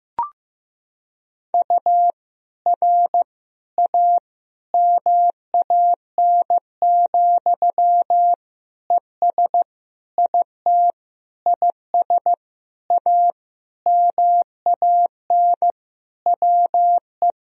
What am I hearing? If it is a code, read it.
Morse code, heard as a keyed tone, about 15 wpm, 700 Hz. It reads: U R A MAN, ES IT IS A MAN WE
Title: Sentences from Top 200 Words and Abbreviations 15wpm